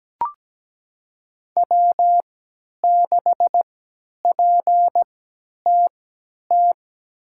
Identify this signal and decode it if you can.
Morse code, heard as a keyed tone, about 17 wpm, spaced as for 10 wpm, 700 Hz. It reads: W6PTT
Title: USA Call Signs - 5 Characters - 3x Character Spacing - 17wpm